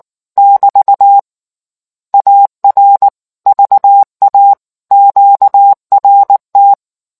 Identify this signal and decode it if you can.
Morse code, heard as a keyed tone, about 19 wpm, 800 Hz. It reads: <BT> AR VA QRT